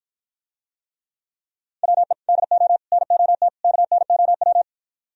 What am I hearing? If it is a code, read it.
Morse code, heard as a keyed tone, about 52 wpm, 700 Hz. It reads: WE DO NOT KNOW